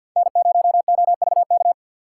Morse code, heard as a keyed tone, about 53 wpm, 700 Hz. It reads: N0OVK
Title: USA Call Signs - 5 Characters - 6x Character Spacing - 50wpm